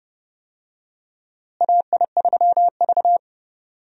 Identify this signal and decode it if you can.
Morse code, heard as a keyed tone, about 30 wpm, 700 Hz. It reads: AI3V
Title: USA Call Signs - 4 Characters - Repeated 2x - 30wpm